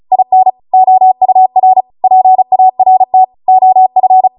Morse code, heard as a keyed tone, about 36 wpm, 750 Hz. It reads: IN OUR PART OF